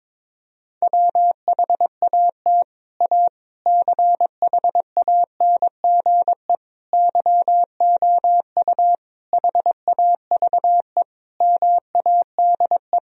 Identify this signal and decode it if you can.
Morse code, heard as a keyed tone, about 22 wpm, 700 Hz. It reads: WHAT A CHANGE YOU HAVE MADE
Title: Sentences from Top 1000 Words - Review Part 1 - 22wpm